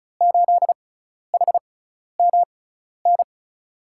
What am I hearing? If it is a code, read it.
Morse code, heard as a keyed tone, about 34 wpm, spaced as for 12 wpm, 700 Hz. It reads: 8HMN
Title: USA Call Signs - 5 Characters - 6x Character Spacing - 35wpm